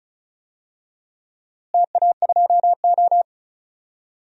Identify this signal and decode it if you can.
Morse code, heard as a keyed tone, about 36 wpm, 700 Hz. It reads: TA2O
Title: DXCC-100 Call Signs - 4 Characters - Repeated 3x - 35wpm